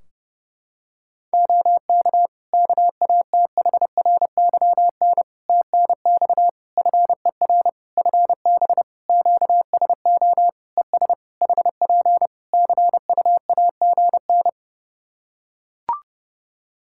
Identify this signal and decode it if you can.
Morse code, heard as a keyed tone, about 30 wpm, 700 Hz. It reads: OK KATHRYN TNX FER FB QSO ES HP CUAGN
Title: K1USN SST QSO Practice 30wpm